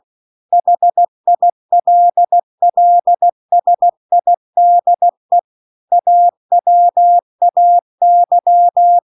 Morse code, heard as a keyed tone, about 16 wpm, 700 Hz. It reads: HILLSIDE AWAY